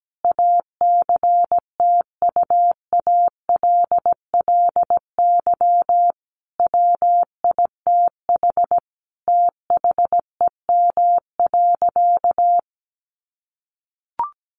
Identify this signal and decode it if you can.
Morse code, heard as a keyed tone, about 17 wpm, 700 Hz. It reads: ACTUALLY WITH THEM.
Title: Sentences from Top 700 Words 17wpm